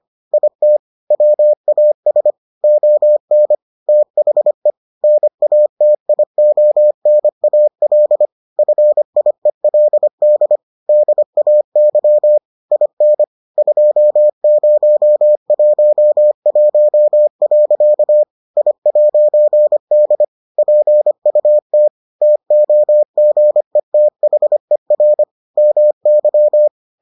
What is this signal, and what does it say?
Morse code, heard as a keyed tone, about 25 wpm, 600 Hz. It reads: IT WAS ON THE NATIONAL FIELD DAY IN 2011. I'D PUT TOGETHER MY